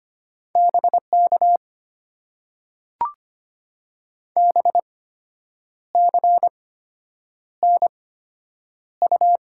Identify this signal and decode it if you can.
Morse code, heard as a keyed tone, about 25 wpm, spaced as for 8 wpm, 700 Hz. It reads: BK BCNU
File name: Top 90 QSO Elements - 8x Character Spacing - 25wpm